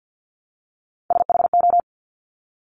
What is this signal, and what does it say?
Morse code, heard as a keyed tone, about 50 wpm, 690 Hz. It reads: SHO
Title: Top Three Letter Combinations in Words 50wpm